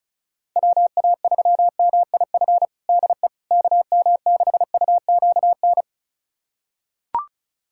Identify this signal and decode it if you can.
Morse code, heard as a keyed tone, about 34 wpm, 700 Hz. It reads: WA3MIF DE KM6UQN
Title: USA Callsign DE Callsign - 6 Characters - 35wpm